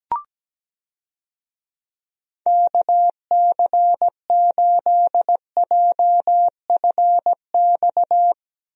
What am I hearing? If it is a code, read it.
Morse code, heard as a keyed tone, about 17 wpm, 700 Hz. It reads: KC8JFX
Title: USA Call Signs - 6 Characters - 17wpm